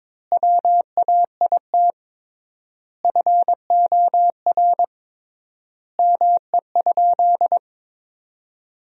Morse code, heard as a keyed tone, about 21 wpm, 700 Hz. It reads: WAIT FOR ME?
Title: 3x Word-Spacing - Sentences from Top 500 Words - 22wpm